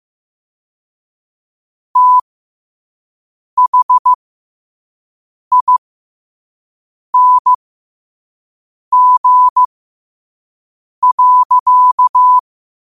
Morse code, heard as a keyed tone, about 15 wpm, spaced as for 5 wpm, 1000 Hz. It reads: THING.
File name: qotdfiles / 5wpm